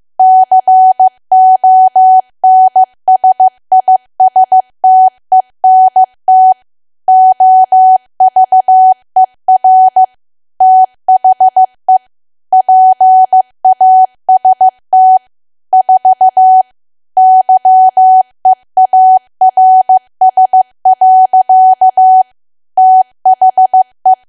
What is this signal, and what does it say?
Morse code, heard as a keyed tone, about 15 wpm, 750 Hz. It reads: CONSISTENT OVER THE PAST 4 YEARS. THE
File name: cwa / W1AW-Level2 / 15WPM